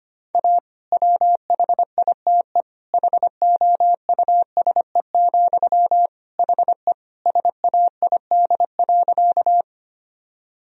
Morse code, heard as a keyed tone, about 25 wpm, 700 Hz. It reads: A WHITE HOUSE, HE SAID.
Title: Sentences from Top 1000 Words - Review Part 1 - 25wpm